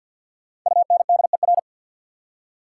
Morse code, heard as a keyed tone, about 62 wpm, 690 Hz. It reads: UNDER
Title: Speed-Racing Prefixes and Suffixes 50wpm